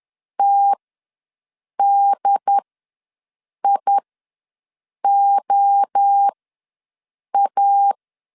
Morse code, heard as a keyed tone, about 10 wpm, spaced as for 6 wpm, 800 Hz. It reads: TDIOA
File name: LEZIONI / 5 LBVC / 5 LBVC ws 10wpm 3733